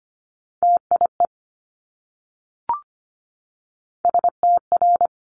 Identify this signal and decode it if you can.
Morse code, heard as a keyed tone, about 25 wpm, 700 Hz. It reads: TIE STR
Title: Multi-letter Phonemes Repeated 3x 25wpm